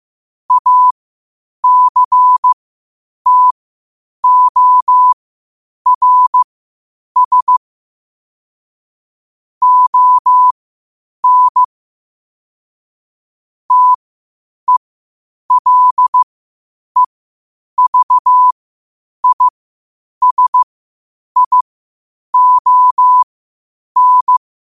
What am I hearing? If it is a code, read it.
Morse code, heard as a keyed tone, about 15 wpm, spaced as for 8 wpm, 1000 Hz. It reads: ACTORS ON TELEVISION
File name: qotdfiles / 8wpm